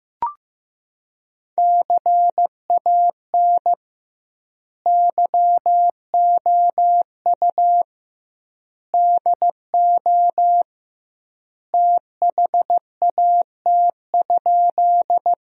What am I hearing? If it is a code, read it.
Morse code, heard as a keyed tone, about 15 wpm, 700 Hz. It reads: CAN YOU DO THAT?
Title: 2x Word-Spacing - Sentences from Top 100 Words - 15wpm